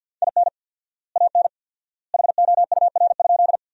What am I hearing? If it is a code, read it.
Morse code, heard as a keyed tone, about 50 wpm, 700 Hz. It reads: IN AN HOUR?
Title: 4x Word-Spacing - Sentences from Top 400 Words - 50wpm